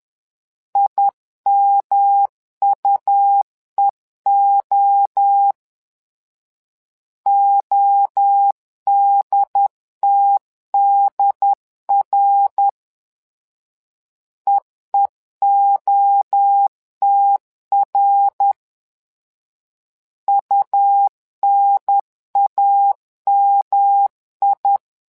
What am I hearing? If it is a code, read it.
Morse code, heard as a keyed tone, about 10 wpm, 800 Hz. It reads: IMUEO ODTDR EEOTR UNAMI
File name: LEZIONI / 3 MWRO / 3 MWRO 10wpm 2148